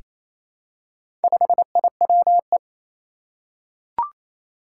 Morse code, heard as a keyed tone, about 28 wpm, 700 Hz. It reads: 5IWE
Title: Four Challenging Letter-Number Combinations 28wpm